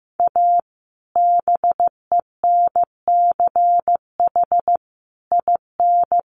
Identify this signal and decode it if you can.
Morse code, heard as a keyed tone, about 15 wpm, 700 Hz. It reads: A BENCH IN
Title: Sentences from Top 1400 Words 15wpm